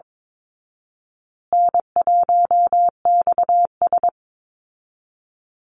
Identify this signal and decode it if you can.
Morse code, heard as a keyed tone, about 22 wpm, 700 Hz. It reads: N1XS
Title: Easy Call Signs Repeated 3x - 22wpm